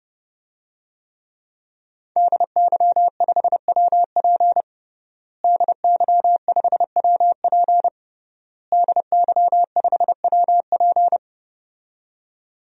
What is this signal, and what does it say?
Morse code, heard as a keyed tone, about 29 wpm, 700 Hz. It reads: DY5WP DY5WP DY5WP
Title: DXCC-100 Call Signs - 5 Characters - Repeated 3x - 30wpm